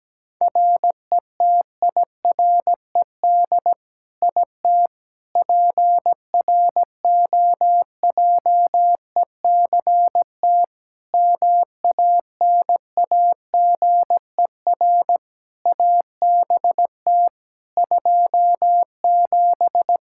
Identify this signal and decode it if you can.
Morse code, heard as a keyed tone, about 17 wpm, 700 Hz. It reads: RETIRED IT PROJECT MANAGER ABT 27
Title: Practice QSO - Exchange 3 - More Info - 17wpm